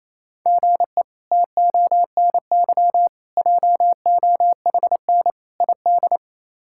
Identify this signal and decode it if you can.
Morse code, heard as a keyed tone, about 28 wpm, 700 Hz. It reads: GE TONY JOHN ID
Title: K1USN SST QSO Practice - 28wpm